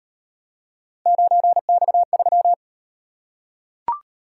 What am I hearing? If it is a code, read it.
Morse code, heard as a keyed tone, about 38 wpm, 700 Hz. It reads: 9X3